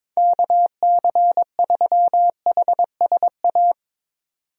Morse code, heard as a keyed tone, about 22 wpm, 700 Hz. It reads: KC3HSA
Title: USA Call Signs - 6 Characters - 22wpm